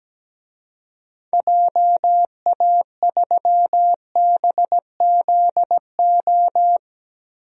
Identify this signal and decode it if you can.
Morse code, heard as a keyed tone, about 17 wpm, 700 Hz. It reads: JA3BZO
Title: DXCC-100 Call Signs - 6 Characters - 17wpm